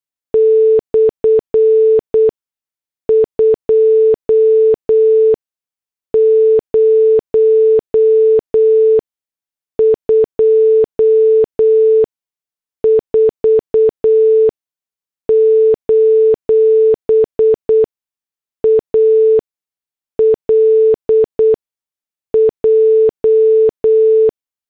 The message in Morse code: /2024: ALJ